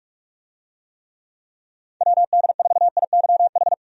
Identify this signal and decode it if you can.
Morse code, heard as a keyed tone, about 46 wpm, 700 Hz. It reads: WD4IYH
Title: USA Call Signs - 6 Characters - 45wpm